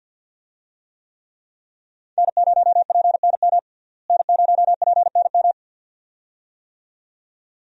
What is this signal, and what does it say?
Morse code, heard as a keyed tone, about 49 wpm, 700 Hz. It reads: N0PNM N0PNM
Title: USA Call Signs - 5 Characters - Repeated 2x - 50wpm